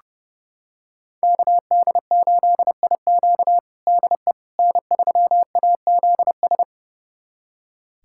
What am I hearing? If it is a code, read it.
Morse code, heard as a keyed tone, about 32 wpm, 700 Hz. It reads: KD8IQ DE N3AZS
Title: USA Callsign DE Callsign - 5 Characters - 30wpm